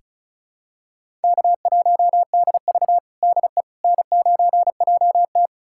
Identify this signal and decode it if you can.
Morse code, heard as a keyed tone, about 34 wpm, 700 Hz. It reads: K1DV DE N9JT
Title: USA Callsign DE Callsign - 4 Characters - 35wpm